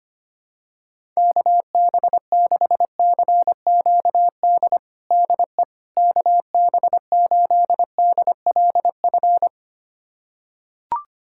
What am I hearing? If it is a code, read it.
Morse code, heard as a keyed tone, about 25 wpm, 700 Hz. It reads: KB6CQD DE KB8DLF
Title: USA Callsign DE Callsign 25wpm